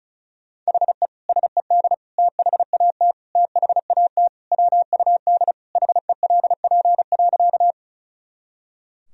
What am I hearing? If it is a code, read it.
Morse code, heard as a keyed tone, about 34 wpm, 690 Hz. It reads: HE SED THAT THAT WUD HELP.
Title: Sentences from Top 200 Words and Abbreviations 35wpm